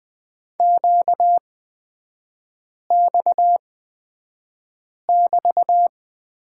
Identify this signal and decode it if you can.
Morse code, heard as a keyed tone, about 20 wpm, spaced as for 5 wpm, 700 Hz. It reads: QX<BT>
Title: 236 Lesson 38e - Letter q - Three Characters 20wpm